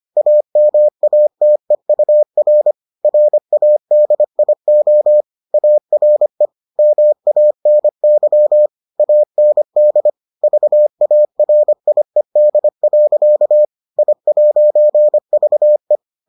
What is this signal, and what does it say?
Morse code, heard as a keyed tone, about 25 wpm, 600 Hz. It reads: AMATEUR RADIO ARE MANY AND VARIED. I'VE